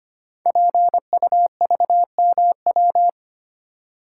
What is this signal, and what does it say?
Morse code, heard as a keyed tone, about 25 wpm, 700 Hz. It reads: PUVMW
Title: Code Groups - 5 Letters - 25wpm